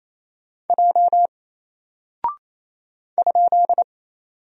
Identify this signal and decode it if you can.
Morse code, heard as a keyed tone, about 27 wpm, 700 Hz. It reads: J ?
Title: Single Character (Letter:Number:Punctuation:Prosign) 28wpm